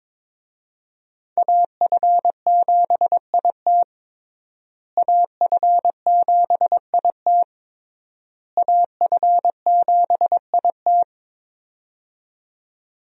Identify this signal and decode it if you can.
Morse code, heard as a keyed tone, about 22 wpm, 700 Hz. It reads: AF7IT AF7IT AF7IT
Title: USA Call Signs - 5 Characters - Repeated 3x - 22wpm